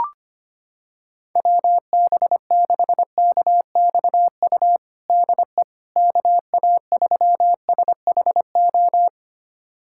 Morse code, heard as a keyed tone, about 25 wpm, 700 Hz. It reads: WB6KXU DE KA3SHO
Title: USA Callsign DE Callsign 25wpm